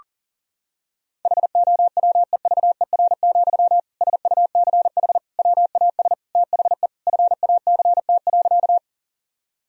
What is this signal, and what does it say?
Morse code, heard as a keyed tone, about 40 wpm, 700 Hz. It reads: HOWEVER, SUCH WAS THE FACT.
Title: Sentences from Top 1000 Words - Review Part 3 - 40wpm